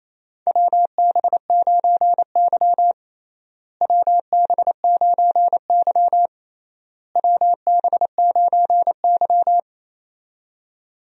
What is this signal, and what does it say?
Morse code, heard as a keyed tone, about 27 wpm, 700 Hz. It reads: WB9Y WB9Y WB9Y
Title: USA Call Signs - 4 Characters - Repeated 3x - 28wpm